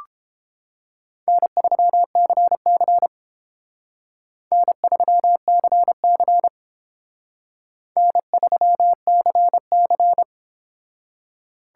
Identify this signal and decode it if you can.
Morse code, heard as a keyed tone, about 29 wpm, 700 Hz. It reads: N3CC N3CC N3CC